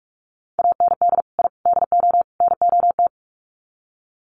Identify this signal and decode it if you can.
Morse code, heard as a keyed tone, about 44 wpm, 700 Hz. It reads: AND I DO NOT
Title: Sets of 4 Words 45wpm